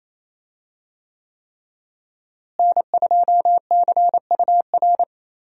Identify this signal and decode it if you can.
Morse code, heard as a keyed tone, about 28 wpm, 700 Hz. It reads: N2CUR
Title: USA Call Signs - 5 Characters - Repeated 3x - 28wpm